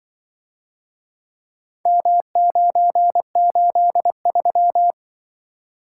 Morse code, heard as a keyed tone, about 24 wpm, 700 Hz. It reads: M983
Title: Speed-Racing - Four Letter-Number Combinations - 20wpm